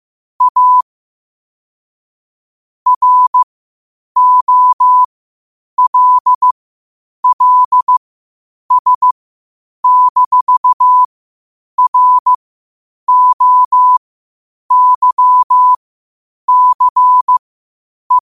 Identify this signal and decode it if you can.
Morse code, heard as a keyed tone, about 15 wpm, spaced as for 8 wpm, 1000 Hz. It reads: A ROLLS-ROYCE
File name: qotdfiles / 8wpm